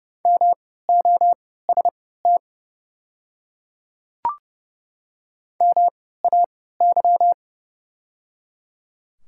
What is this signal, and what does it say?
Morse code, heard as a keyed tone, about 31 wpm, spaced as for 17 wpm, 700 Hz. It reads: MOST MAY
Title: 3x Character Spacing - Top 100 Words - 30wpm